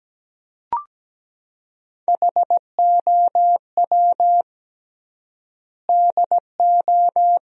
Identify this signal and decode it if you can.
Morse code, heard as a keyed tone, about 17 wpm, 700 Hz. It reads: HOW DO
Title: 3x Word-Spacing - Sentences from Top 400 Words - 17wpm